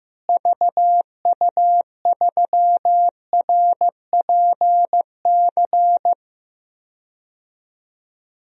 Morse code, heard as a keyed tone, about 15 wpm, 700 Hz. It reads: VU3RPC
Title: DXCC-100 Call Signs - 6 Characters - Repeated 2x - 15wpm